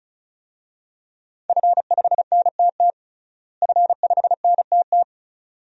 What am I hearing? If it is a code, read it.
Morse code, heard as a keyed tone, about 34 wpm, 700 Hz. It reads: F5NTT F5NTT
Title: DXCC-100 Call Signs - 5 Characters - Repeated 2x - 35wpm